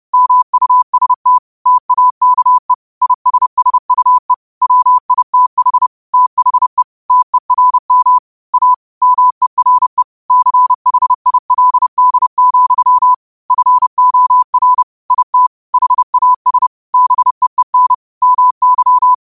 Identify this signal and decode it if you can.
Morse code, heard as a keyed tone, about 30 wpm, 1000 Hz. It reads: MUST TAKE ISSUE WITH THE TERM A MERE CHILD, FOR IT HAS BEEN MY
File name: qotdfiles / 30wpm